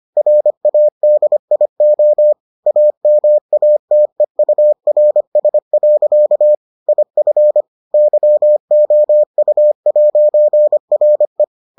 Morse code, heard as a keyed tone, about 25 wpm, 600 Hz. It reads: RADIO AMATEURS. IF YOU'RE